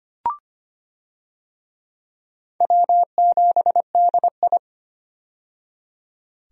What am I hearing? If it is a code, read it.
Morse code, heard as a keyed tone, about 25 wpm, 700 Hz. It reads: W7DI
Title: USA Call Signs - 4 Characters - 25wpm